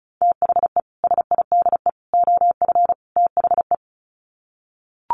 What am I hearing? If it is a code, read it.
Morse code, heard as a keyed tone, about 34 wpm, 700 Hz. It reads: THE SIDE OF THE
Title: Sets of 4 Words 35wpm